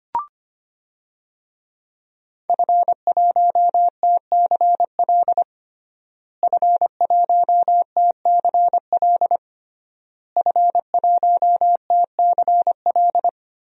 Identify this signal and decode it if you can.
Morse code, heard as a keyed tone, about 25 wpm, 700 Hz. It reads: F1TCL F1TCL F1TCL
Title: DXCC-100 Call Signs - 5 Characters - Repeated 3x - 25wpm